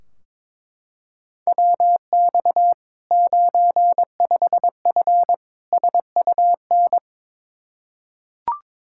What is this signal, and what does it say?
Morse code, heard as a keyed tone, about 22 wpm, 700 Hz. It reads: WX 95F SUN
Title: Practice QSO - Exchange 2 - Info - 22wpm